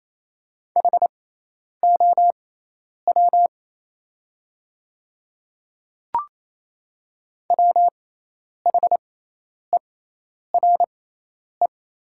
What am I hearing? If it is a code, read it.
Morse code, heard as a keyed tone, about 28 wpm, spaced as for 10 wpm, 700 Hz. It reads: HOW WHERE
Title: 6x Character Spacing - Top 300 Words - 28wpm